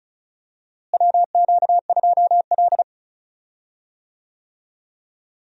Morse code, heard as a keyed tone, about 36 wpm, 700 Hz. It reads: WQ2L
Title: USA Call Signs - 4 Characters - Repeated 2x - 35wpm